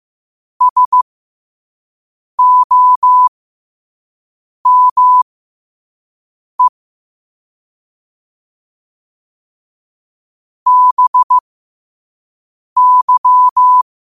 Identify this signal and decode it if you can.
Morse code, heard as a keyed tone, about 15 wpm, spaced as for 5 wpm, 1000 Hz. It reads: SOME BY